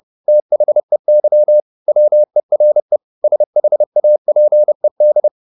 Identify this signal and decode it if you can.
Morse code, heard as a keyed tone, about 30 wpm, 600 Hz. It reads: THEY WERE SHAPED